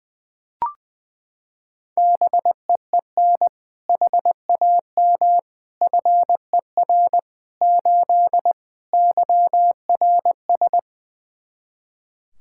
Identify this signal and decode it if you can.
Morse code, heard as a keyed tone, about 20 wpm, 700 Hz. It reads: BEEN HAM FER 8 YRS
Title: Practice QSO - Exchange 3 - More Info - 20wpm